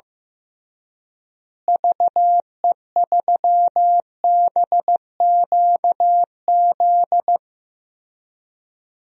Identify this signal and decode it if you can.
Morse code, heard as a keyed tone, about 15 wpm, 700 Hz. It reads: VE3BQZ
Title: DXCC-100 Call Signs - 6 Characters - Repeated 3x - 15wpm